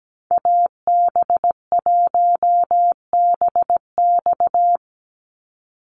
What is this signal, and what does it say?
Morse code, heard as a keyed tone, about 17 wpm, 700 Hz. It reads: AB1BX
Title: Call Signs Repeated 3x - 17wpm